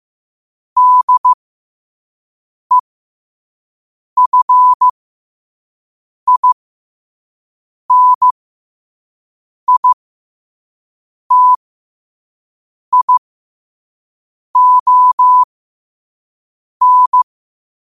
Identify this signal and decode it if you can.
Morse code, heard as a keyed tone, about 15 wpm, spaced as for 5 wpm, 1000 Hz. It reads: DEFINITION
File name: qotdfiles / 5wpm